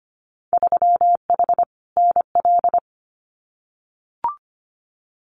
Morse code, heard as a keyed tone, about 25 wpm, 700 Hz. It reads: 3H NL
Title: Winter Field Day Exchanges Encore 25wpm